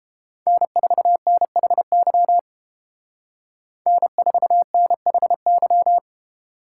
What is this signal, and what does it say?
Morse code, heard as a keyed tone, about 30 wpm, 700 Hz. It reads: N4NHY N4NHY